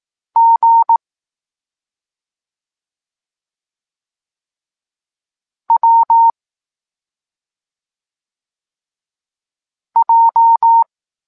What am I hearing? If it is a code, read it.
Morse code, heard as a keyed tone, about 18 wpm, 920 Hz. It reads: G W J
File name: Tekst-26 1